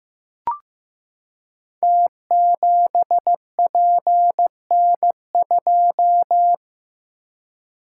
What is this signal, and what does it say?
Morse code, heard as a keyed tone, about 15 wpm, 700 Hz. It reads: T7PN2
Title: Code Groups - 5 Letter:Numbers - 15wpm